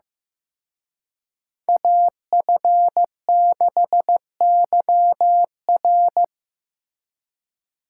Morse code, heard as a keyed tone, about 18 wpm, 700 Hz. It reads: AF6YR
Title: Speed Racing - USA Call Signs 15wpm